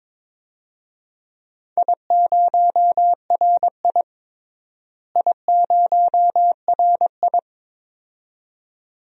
Morse code, heard as a keyed tone, about 22 wpm, 700 Hz. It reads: I0RI I0RI
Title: DXCC-100 Call Signs - 4 Characters - Repeated 2x - 22wpm